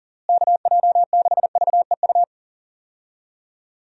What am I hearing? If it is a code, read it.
Morse code, heard as a keyed tone, about 40 wpm, 700 Hz. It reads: KJ6VEU
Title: USA Call Signs - 6 Characters - Repeated 3x - 40wpm